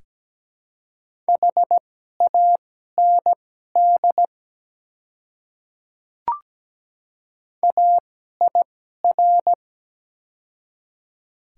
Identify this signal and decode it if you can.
Morse code, heard as a keyed tone, about 17 wpm, spaced as for 12 wpm, 700 Hz. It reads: HAND AIR
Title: 2x Character Spacing - Top 200 Words - 17wpm